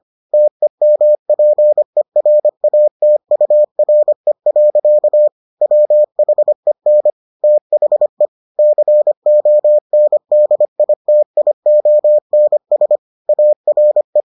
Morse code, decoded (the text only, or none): TEMPERATURE. WHEN THE CONDITIONS ARE